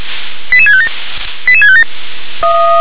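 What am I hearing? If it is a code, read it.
Five-tone selective call (ZVEI), heard as quick runs of five tones, tones of about 70 ms. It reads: 88657 88657
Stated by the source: Five-tone ZVEI call 88724, received by SDR.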